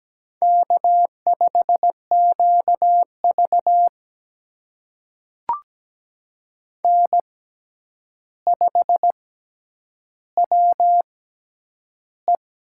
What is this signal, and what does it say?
Morse code, heard as a keyed tone, about 17 wpm, spaced as for 9 wpm, 700 Hz. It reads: K5QV N5WE
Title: USA Call Signs - 6x Character Spacing - 17wpm